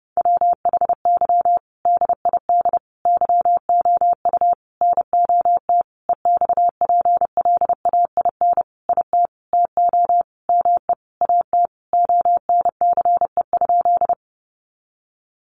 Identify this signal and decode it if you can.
Morse code, heard as a keyed tone, about 30 wpm, 700 Hz. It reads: WHY DID YOU NOT EXPLAIN IT TO ME AT ONCE?
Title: Sentences from Top 1100 Words 30wpm